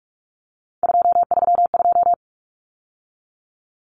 Morse code, heard as a keyed tone, about 44 wpm, 700 Hz. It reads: 232
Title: Numbers - 3-Digit 45wpm